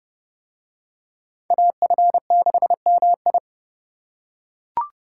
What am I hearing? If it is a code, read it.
Morse code, heard as a keyed tone, about 27 wpm, 700 Hz. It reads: AF6MI